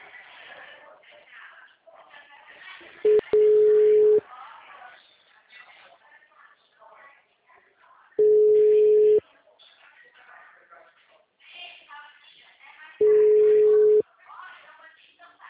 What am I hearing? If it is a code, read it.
Ringback tone (Europe, 425 Hz), with speech.